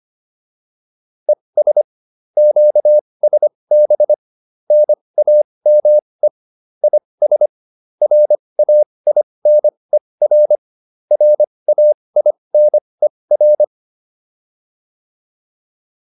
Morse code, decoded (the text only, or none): ES QSB NAME IS RAINER RAINER